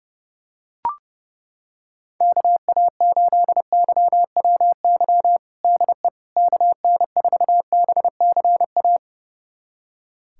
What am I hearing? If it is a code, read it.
Morse code, heard as a keyed tone, about 30 wpm, 700 Hz. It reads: KA8YWY DE KN4BCA
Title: USA Callsign DE Callsign - 6 Characters - 30wpm